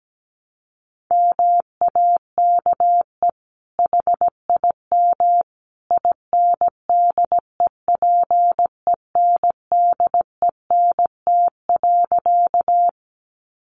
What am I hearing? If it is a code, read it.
Morse code, heard as a keyed tone, about 17 wpm, 700 Hz. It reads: MAKE HIM INDEPENDENT.
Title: Sentences from Top 900 Words 17wpm